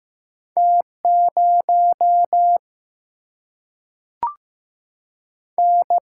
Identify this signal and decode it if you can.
Morse code, heard as a keyed tone, about 15 wpm, 700 Hz. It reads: T 0 N
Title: Single-Letter Single-Number 15wpm